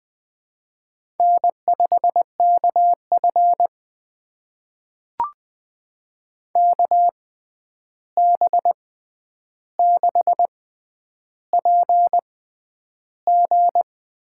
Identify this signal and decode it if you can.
Morse code, heard as a keyed tone, about 20 wpm, spaced as for 10 wpm, 700 Hz. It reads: N5KF KB6PG
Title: USA Call Signs - 6x Character Spacing - 20wpm